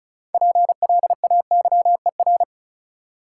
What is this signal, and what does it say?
Morse code, heard as a keyed tone, about 36 wpm, 700 Hz. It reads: PLAYER
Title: Sentences from Top 1000 Words - Review Part 4 - New Word Drill - 35wpm